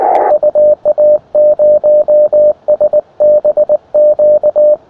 Morse code, heard as a keyed tone, about 19 wpm, 600 Hz. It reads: UA0SBQ
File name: attachments / ua0sbq 20m